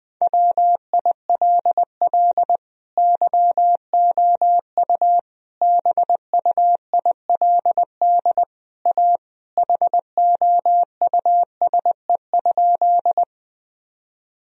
Morse code, heard as a keyed tone, about 20 wpm, 700 Hz. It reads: WILL YOU BUILD A HOUSE?
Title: Sentences from Top 1000 Words - Review Part 1 - 20wpm